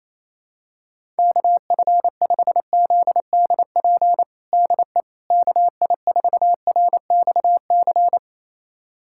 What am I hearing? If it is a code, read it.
Morse code, heard as a keyed tone, about 28 wpm, 700 Hz. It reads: KF5ZDP DE KI4RXC
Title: USA Callsign DE Callsign 28wpm